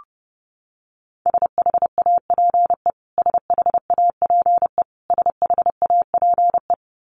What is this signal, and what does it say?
Morse code, heard as a keyed tone, about 30 wpm, 700 Hz. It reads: SHAPE SHAPE SHAPE
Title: Top 500 Words Repeated 30wpm